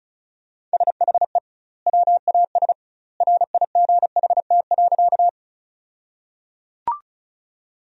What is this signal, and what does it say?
Morse code, heard as a keyed tone, about 36 wpm, 700 Hz. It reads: SHE WAS RIGHT.
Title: 2x Word-Spacing - Sentences from Top 200 Words - 35wpm